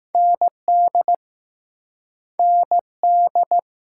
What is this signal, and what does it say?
Morse code, heard as a keyed tone, about 20 wpm, 700 Hz. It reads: ND ND